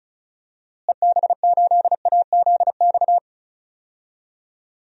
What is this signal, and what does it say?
Morse code, heard as a keyed tone, about 36 wpm, 700 Hz. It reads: EB8AZX
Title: DXCC-100 Call Signs - 6 Characters - Repeated 2x - 35wpm